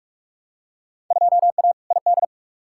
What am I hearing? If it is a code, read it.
Morse code, heard as a keyed tone, about 42 wpm, 700 Hz. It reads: 2A ID
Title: Field Day Exchanges Encore - 45wpm